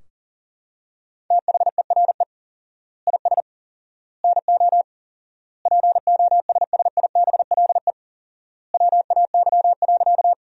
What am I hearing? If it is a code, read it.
Morse code, heard as a keyed tone, about 40 wpm, 700 Hz. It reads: THERE IS NO POSSIBLE WAY.
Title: 4x Word-Spacing - Sentences from Top 500 Words - 40wpm